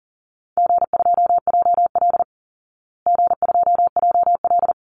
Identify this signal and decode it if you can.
Morse code, heard as a keyed tone, about 37 wpm, 700 Hz. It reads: G2JL G2JL
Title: Easy Call Signs Repeated 2x - 40wpm